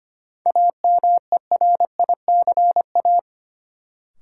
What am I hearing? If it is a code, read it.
Morse code, heard as a keyed tone, about 25 wpm, 700 Hz. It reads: AMERICA
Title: Sets of 5 Words - New Word Drill - 25wpm